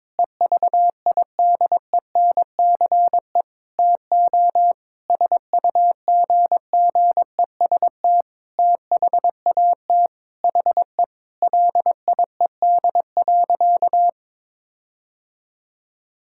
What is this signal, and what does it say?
Morse code, heard as a keyed tone, about 22 wpm, 700 Hz. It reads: EVIDENCE TO SUGGEST THAT HE LIED.
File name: Sentences with 7-Word Phrases - 22wpm